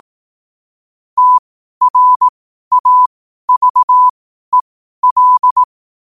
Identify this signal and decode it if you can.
Morse code, heard as a keyed tone, about 18 wpm, spaced as for 12 wpm, 1000 Hz. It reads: TRAVEL
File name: qotdfiles / 12wpm